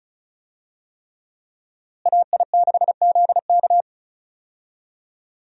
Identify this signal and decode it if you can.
Morse code, heard as a keyed tone, about 36 wpm, 700 Hz. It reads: AI6ZK
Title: USA Call Signs - 5 Characters - Repeated 3x - 35wpm